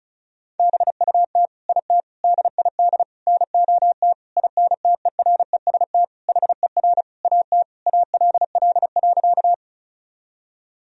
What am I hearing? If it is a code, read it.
Morse code, heard as a keyed tone, about 36 wpm, 700 Hz. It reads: BUT IT DID NOT INTEREST HER AT ALL.
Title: Sentences from Top 1000 Words - Review Part 2 - 35wpm